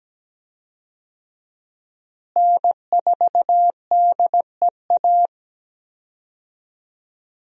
Morse code, heard as a keyed tone, about 17 wpm, 700 Hz. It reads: N4DEA
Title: USA Call Signs - 5 Characters - Repeated 3x - 17wpm